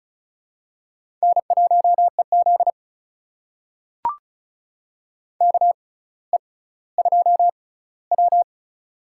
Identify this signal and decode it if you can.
Morse code, heard as a keyed tone, about 34 wpm, spaced as for 18 wpm, 700 Hz. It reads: N1EZ KE2W
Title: USA Call Signs - 4 Characters - 6x Character Spacing - 35wpm